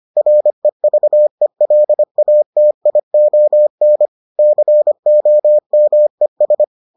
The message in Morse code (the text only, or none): REVELATION COMES